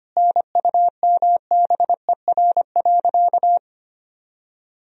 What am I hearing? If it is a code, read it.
Morse code, heard as a keyed tone, about 25 wpm, 700 Hz. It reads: NUMBER.
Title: Sentences from Top 1000 Words - Review Part 1 - 25wpm